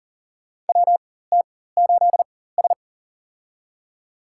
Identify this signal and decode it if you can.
Morse code, heard as a keyed tone, about 38 wpm, spaced as for 18 wpm, 700 Hz. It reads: WT8S
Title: USA Call Signs - 4 Characters - 4x Character Spacing - 40wpm